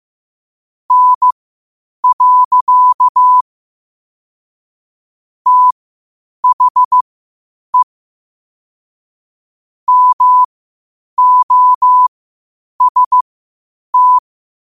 Morse code, heard as a keyed tone, about 15 wpm, spaced as for 8 wpm, 1000 Hz. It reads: N. THE MOST
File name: qotdfiles / 8wpm